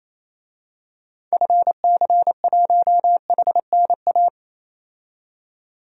Morse code, heard as a keyed tone, about 28 wpm, 700 Hz. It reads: FC1HNA
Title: DXCC-100 Call Signs - 6 Characters - 28wpm